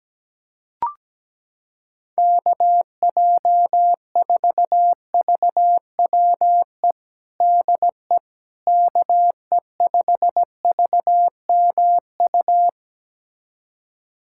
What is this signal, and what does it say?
Morse code, heard as a keyed tone, about 17 wpm, 700 Hz. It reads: KJ4VWE DE KE5VMU
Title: USA Callsign DE Callsign - 6 Characters - 17wpm